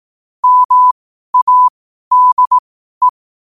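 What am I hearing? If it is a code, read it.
Morse code, heard as a keyed tone, about 18 wpm, spaced as for 12 wpm, 1000 Hz. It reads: MADE
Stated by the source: Morse code sent at 12 words per minute.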